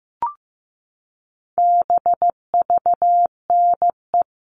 Morse code, heard as a keyed tone, about 15 wpm, 700 Hz. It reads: BVNE
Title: All Cut Numbers - 4-Digit 15wpm